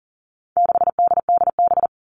Morse code, heard as a keyed tone, about 40 wpm, 700 Hz. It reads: 6DDB
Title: All Cut Numbers - 4-Digit 40wpm